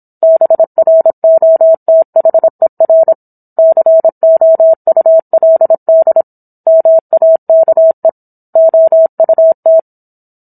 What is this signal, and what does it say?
Morse code, heard as a keyed tone, about 27 wpm, 650 Hz. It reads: BROTHER COULD MAKE OUT
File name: Book One / 17-Chapter 17 - 26 WPM